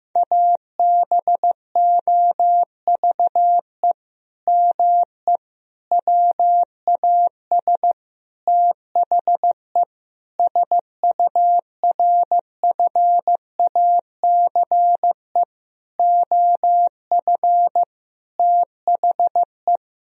Morse code, heard as a keyed tone, about 15 wpm, 700 Hz. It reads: ABOVE ME WAS THE SURFACE OF THE
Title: Sentences from Top 1000 Words - Review Part 3 - 15wpm